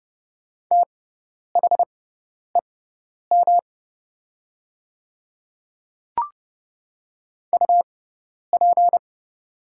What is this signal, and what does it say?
Morse code, heard as a keyed tone, about 31 wpm, spaced as for 10 wpm, 700 Hz. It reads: THEM UP
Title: 6x Character Spacing - Top 100 Words - 30wpm